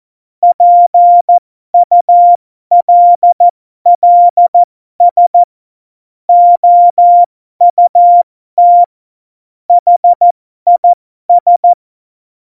Morse code, heard as a keyed tone, about 14 wpm, spaced as for 12 wpm, 700 Hz. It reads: PULLS OUT HIS
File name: Morsekurs / Witze